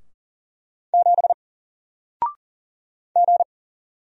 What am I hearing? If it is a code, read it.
Morse code, heard as a keyed tone, about 40 wpm, 700 Hz. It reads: 7 G
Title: Single Letter-Number Alt 40wpm